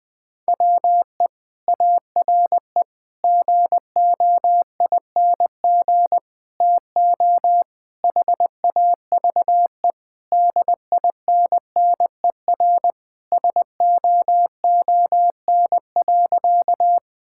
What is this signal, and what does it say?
Morse code, heard as a keyed tone, about 21 wpm, 700 Hz. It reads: WE ARE GOING TO HAVE DINNER SOON.
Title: Sentences with 3-Word Phrases - 20wpm